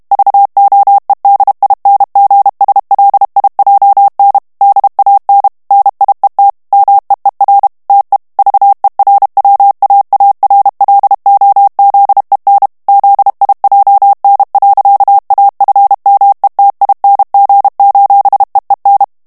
Morse code, heard as a keyed tone, about 32 wpm, 800 Hz. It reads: VOEDINGSLIJN DAN NIET MEER TE VERWAARLOZEN ZIJN. AFMETING: EEN
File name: morse mp3 / HOLLANDAIS texte clair 1 vitesse 30 mots minute